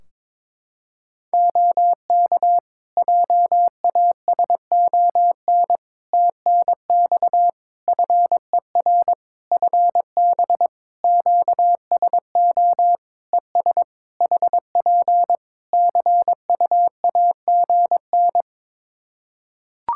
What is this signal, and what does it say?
Morse code, heard as a keyed tone, about 22 wpm, 700 Hz. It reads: OK JASON TNX FER FB QSO ES HP CUAGN
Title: Practice QSO - Exchange 4 - Ending - 22wpm